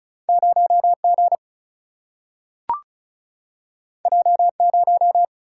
Morse code, heard as a keyed tone, about 36 wpm, 700 Hz. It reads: 0G J0
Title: Single-Letter Single-Number 35wpm